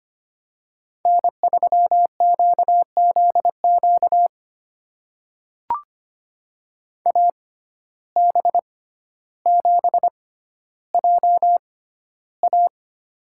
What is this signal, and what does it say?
Morse code, heard as a keyed tone, about 25 wpm, spaced as for 13 wpm, 700 Hz. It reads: N3QZQ AB7JA
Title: USA Call Signs - 5 Characters - 6x Character Spacing - 25wpm